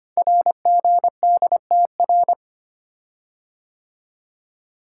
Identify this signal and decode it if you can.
Morse code, heard as a keyed tone, about 25 wpm, 700 Hz. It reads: RGDTR
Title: Code Groups - 5 Letters - 25wpm